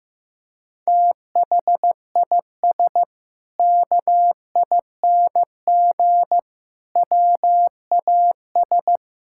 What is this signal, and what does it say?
Morse code, heard as a keyed tone, about 15 wpm, 700 Hz. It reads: THIS KING WAS
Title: Sentences from Top 1600 Words 15wpm